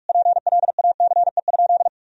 Morse code, heard as a keyed tone, about 46 wpm, 700 Hz. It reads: PLACE?
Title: Sentences from Top 200 Words and Abbreviations 45wpm